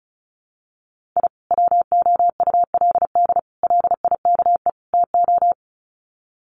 Morse code, heard as a keyed tone, about 34 wpm, 700 Hz. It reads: I WOULD LIKE TO
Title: Sets of 4 Words 35wpm